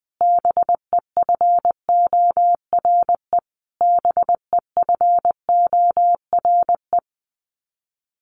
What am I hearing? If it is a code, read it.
Morse code, heard as a keyed tone, about 20 wpm, 700 Hz. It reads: BEFORE BEFORE
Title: Top 200 Words Repeated 20wpm